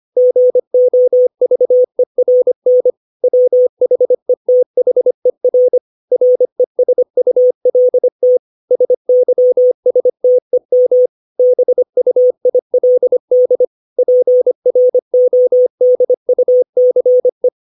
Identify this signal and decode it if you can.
Morse code, heard as a keyed tone, about 25 wpm, 500 Hz. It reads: GOVERN WHETHER RESULT SYSTEM BUILD PRODUCE
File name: 632 words / 632 Words 25WPM